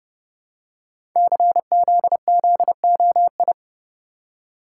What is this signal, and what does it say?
Morse code, heard as a keyed tone, about 30 wpm, 700 Hz. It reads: CZZOI
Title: Code Groups - 5 Letters - 30wpm